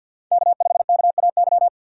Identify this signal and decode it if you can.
Morse code, heard as a keyed tone, about 50 wpm, 700 Hz. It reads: K5KAY
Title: USA Call Signs - 5 Characters - 6x Character Spacing - 50wpm